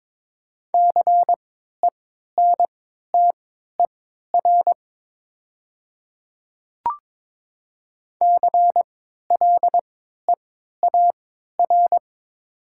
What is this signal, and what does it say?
Morse code, heard as a keyed tone, about 22 wpm, spaced as for 13 wpm, 700 Hz. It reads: CENTER CLEAR
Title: 3x Character Spacing - Top 500 Words - 22wpm